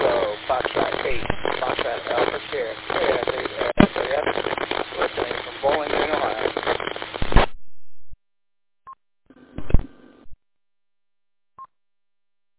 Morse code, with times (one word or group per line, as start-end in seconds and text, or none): none